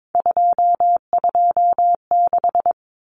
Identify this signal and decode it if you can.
Morse code, heard as a keyed tone, about 22 wpm, 700 Hz. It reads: 226
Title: Numbers - 3-Digit 22wpm